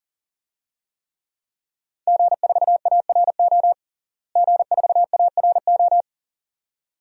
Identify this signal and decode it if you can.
Morse code, heard as a keyed tone, about 40 wpm, 700 Hz. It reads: G4ARO G4ARO
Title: DXCC-100 Call Signs - 5 Characters - Repeated 2x - 40wpm